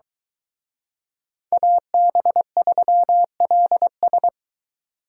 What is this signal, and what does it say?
Morse code, heard as a keyed tone, about 23 wpm, 700 Hz. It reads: AB3LS